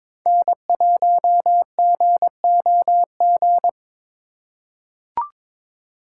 Morse code, heard as a keyed tone, about 22 wpm, 700 Hz. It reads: N1GOG
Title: USA Call Signs - 2x Character Spacing - 22wpm